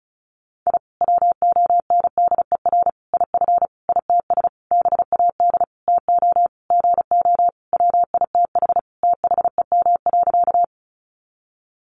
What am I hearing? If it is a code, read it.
Morse code, heard as a keyed tone, about 34 wpm, 700 Hz. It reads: I WONDER IF ITS BAD TO GO WITH THEM.
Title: Sentences from Top 700 Words 35wpm